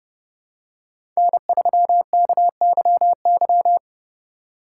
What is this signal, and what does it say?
Morse code, heard as a keyed tone, about 30 wpm, 700 Hz. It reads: N3KYY
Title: USA Call Signs - 5 Characters - 4x Character Spacing - 30wpm